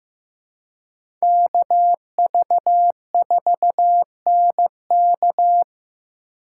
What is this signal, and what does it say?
Morse code, heard as a keyed tone, about 16 wpm, 700 Hz. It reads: KV4NK
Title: USA Call Signs - 4x Character Spacing - 15wpm